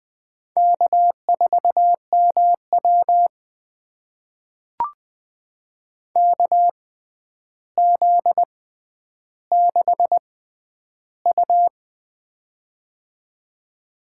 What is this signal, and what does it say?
Morse code, heard as a keyed tone, about 20 wpm, spaced as for 10 wpm, 700 Hz. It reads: K4MW KZ6U
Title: USA Call Signs - 4 Characters - 6x Character Spacing - 20wpm